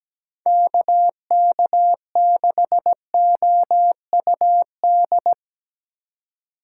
Morse code, heard as a keyed tone, about 17 wpm, 700 Hz. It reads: KK6OUD
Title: USA Call Signs - 6 Characters - 2x Character Spacing - 17wpm